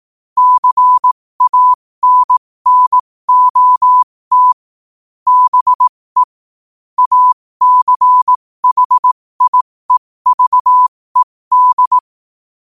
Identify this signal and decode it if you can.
Morse code, heard as a keyed tone, about 18 wpm, spaced as for 15 wpm, 1000 Hz. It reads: CANNOT BE ACHIEVED
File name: qotdfiles / 15wpm